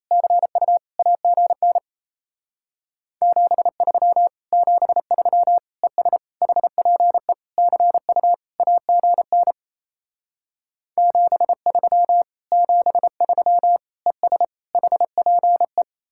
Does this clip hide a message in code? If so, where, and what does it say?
Yes: Morse code from 0.1 to 15.8 s, heard as a keyed tone, about 37 wpm, 700 Hz. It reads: CU AGN 73 73 ES HPE CU AGN 73 73 ES HPE